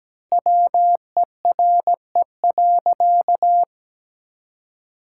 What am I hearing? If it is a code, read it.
Morse code, heard as a keyed tone, about 17 wpm, 700 Hz. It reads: WERE.
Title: Sentences from Top 1000 Words - Review Part 6 - 17wpm